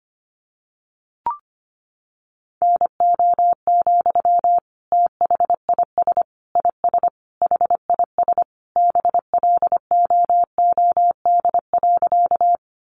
Morse code, heard as a keyed tone, about 25 wpm, 700 Hz. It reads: NO, THIS IS HIS BLOOD.
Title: Sentences from Top 1100 Words 25wpm